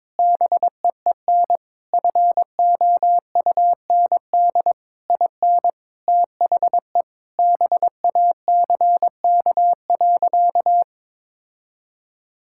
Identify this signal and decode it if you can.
Morse code, heard as a keyed tone, about 22 wpm, 700 Hz. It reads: BEEN FOUND IN THE BACK.
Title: Sentences from Top 1000 Words - Review Part 5 - 22wpm